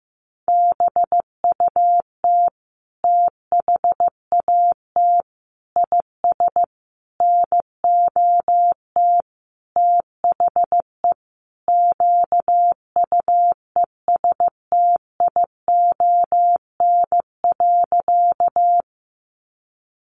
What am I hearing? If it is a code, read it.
Morse code, heard as a keyed tone, about 15 wpm, 700 Hz. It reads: BUT THAT IS NOT THE QUESTION.
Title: Sentences from Top 600 Words 15wpm